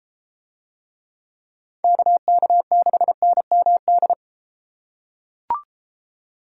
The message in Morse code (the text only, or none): KK6NMD